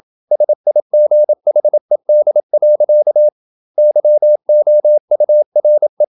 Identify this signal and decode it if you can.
Morse code, heard as a keyed tone, about 28 wpm, 600 Hz. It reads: SIGHED. YOURE
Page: Never 26wpm | CW med Gnister